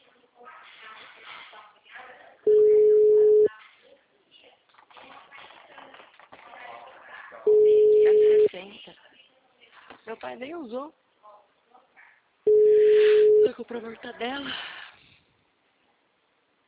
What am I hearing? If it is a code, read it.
Ringback tone (Europe, 425 Hz), with speech.
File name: out- -2055- - - .53906